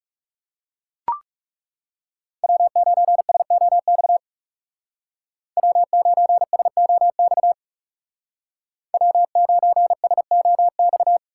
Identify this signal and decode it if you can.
Morse code, heard as a keyed tone, about 37 wpm, 700 Hz. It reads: W9SOX W9SOX W9SOX
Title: Speed Racing - USA 1x3 Call Signs 30wpm